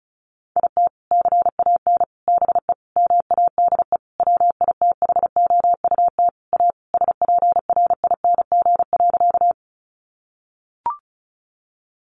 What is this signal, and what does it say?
Morse code, heard as a keyed tone, about 34 wpm, 700 Hz. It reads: IT CAN BE MADE WITHOUT A SPRING.
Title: Sentences from Top 1200 Words 35wpm